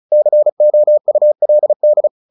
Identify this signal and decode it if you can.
Morse code, heard as a keyed tone, about 34 wpm, 600 Hz. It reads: COULD
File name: the old man and the sea / the old man and the sea 1 35wpm